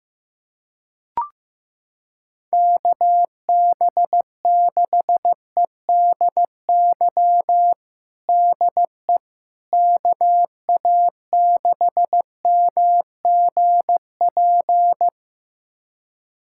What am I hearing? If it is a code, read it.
Morse code, heard as a keyed tone, about 16 wpm, 700 Hz. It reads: KB6EDY DE KA6MGP
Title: USA Callsign DE Callsign - 6 Characters - 15wpm